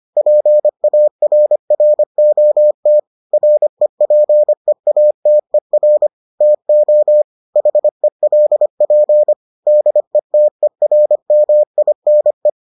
Morse code, heard as a keyed tone, about 25 wpm, 600 Hz. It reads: PARROT REPEATER TO HELP DETERMINE